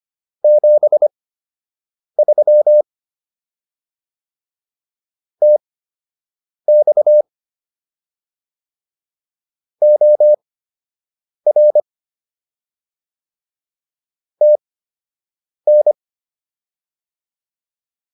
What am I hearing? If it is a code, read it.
Morse code, heard as a keyed tone, about 25 wpm, spaced as for 7 wpm, 600 Hz. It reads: 73 TX OR TN